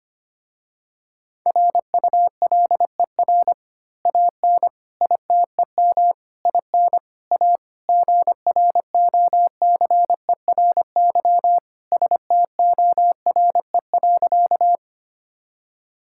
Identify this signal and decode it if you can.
Morse code, heard as a keyed tone, about 25 wpm, 700 Hz. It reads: RULER AN ITEM IN A GROCERY STORE.
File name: practice / 25wpm / Easy Words with Hints - 25wpm